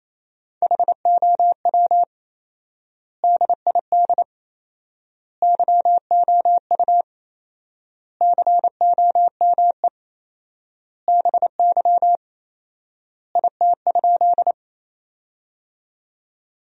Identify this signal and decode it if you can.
Morse code, heard as a keyed tone, about 27 wpm, 700 Hz. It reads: HOW DID YOU COME BY IT?
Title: 4x Word-Spacing - Sentences from Top 100 Words - 28wpm